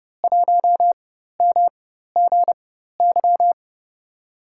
Morse code, heard as a keyed tone, about 30 wpm, spaced as for 14 wpm, 700 Hz. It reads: 1MGY
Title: USA Call Signs - 4x Character Spacing - 30wpm